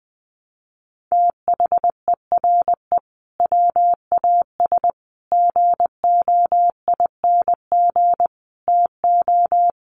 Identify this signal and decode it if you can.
Morse code, heard as a keyed tone, about 20 wpm, 700 Hz. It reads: THERE WAS GOING TO
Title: Sets of 5 Words Encore2 20wpm